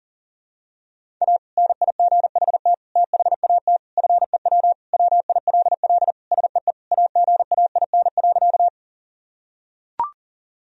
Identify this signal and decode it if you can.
Morse code, heard as a keyed tone, about 40 wpm, 700 Hz. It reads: A NIGHT THAT FEW WILL SEE AGAIN.
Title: Sentences from Top 1000 Words - Review Part 1 - 40wpm